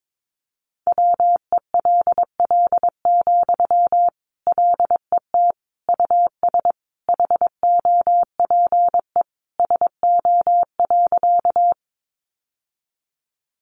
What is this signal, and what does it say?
Morse code, heard as a keyed tone, about 22 wpm, 700 Hz. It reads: WELL, LET US HOPE SO.
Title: Sentences from Top 1000 Words 22wpm